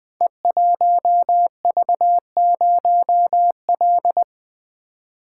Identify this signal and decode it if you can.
Morse code, heard as a keyed tone, about 20 wpm, 700 Hz. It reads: E1V0L
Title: Code Groups - 5 Letter:Numbers - 20wpm